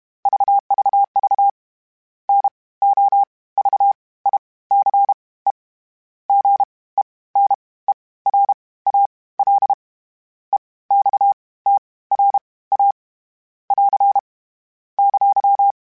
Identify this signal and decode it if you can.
Morse code, heard as a keyed tone, about 32 wpm, spaced as for 19 wpm, 800 Hz. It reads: VVV NOVICE GENERAL EXTRA <AR> !
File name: (spaced) / FromSMs spaced mp3s / Classes sp@30